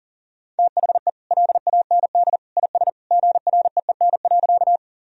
Morse code, heard as a keyed tone, about 40 wpm, 700 Hz. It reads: THE LAND IS GREEN.
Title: Sentences from Top 1000 Words - Review Part 2 - 40wpm